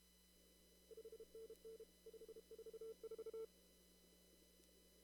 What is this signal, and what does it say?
Morse code, heard as a keyed tone, about 31 wpm, 450 Hz. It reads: 5NN 544
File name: CW / kt5j wpx cw-op-k5pi